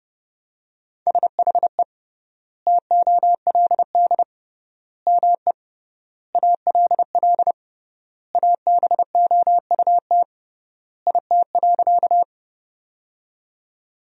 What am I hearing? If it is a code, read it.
Morse code, heard as a keyed tone, about 30 wpm, 700 Hz. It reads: SHE TOLD ME ALL ABOUT IT.
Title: 3x Word-Spacing - Sentences from Top 400 Words - 30wpm